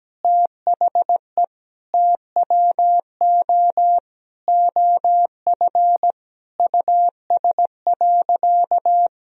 Morse code, heard as a keyed tone, about 17 wpm, 700 Hz. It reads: THE TWO OF US.
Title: Sentences from Top 1000 Words - Review Part 1 - 17wpm